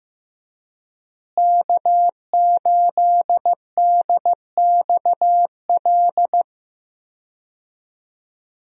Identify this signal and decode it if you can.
Morse code, heard as a keyed tone, about 15 wpm, 700 Hz. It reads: K8DXL
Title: USA Call Signs - 5 Characters - Repeated 3x - 15wpm